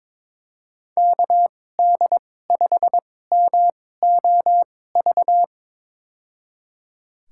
Morse code, heard as a keyed tone, about 22 wpm, spaced as for 16 wpm, 700 Hz. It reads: KD5MOV
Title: USA Call Signs - 6 Characters - 2x Character Spacing - 22wpm